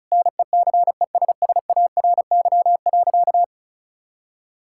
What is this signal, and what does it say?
Morse code, heard as a keyed tone, about 34 wpm, 700 Hz. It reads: NECESSARY.
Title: Sentences from Top 1000 Words - Review Part 5 - 35wpm